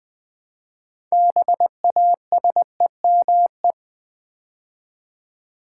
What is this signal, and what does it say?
Morse code, heard as a keyed tone, about 20 wpm, 700 Hz. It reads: BASEME
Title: Build A Word - 3 Letters At A Time - 8-Letter Words - 20wpm